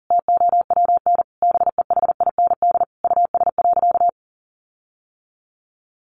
Morse code, heard as a keyed tone, about 40 wpm, 700 Hz. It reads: TOWN BEHIND US.
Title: Sentences from Top 1300 Words 40wpm